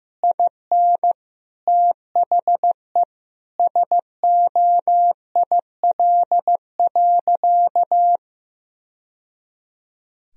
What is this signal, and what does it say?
Morse code, heard as a keyed tone, about 16 wpm, 700 Hz. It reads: IN THE SOIL.
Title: Sentences from Top 1000 Words - Review Part 4 - 15wpm